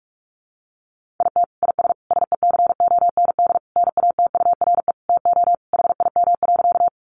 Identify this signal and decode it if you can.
Morse code, heard as a keyed tone, about 44 wpm, 700 Hz. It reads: IT IS SECOND NATURE TO HIM.
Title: Sentences from Top 600 Words 45wpm